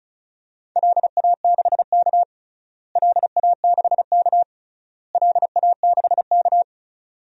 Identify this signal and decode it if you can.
Morse code, heard as a keyed tone, about 36 wpm, 690 Hz. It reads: LA6K LA6K LA6K
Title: DXCC-100 Call Signs - 4 Characters - Repeated 3x - 35wpm